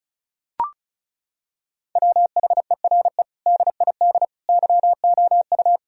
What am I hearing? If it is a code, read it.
Morse code, heard as a keyed tone, about 34 wpm, 700 Hz. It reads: WHERE DID YOU
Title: Sentences from Top 1000 Words - Review Part 4 - 35wpm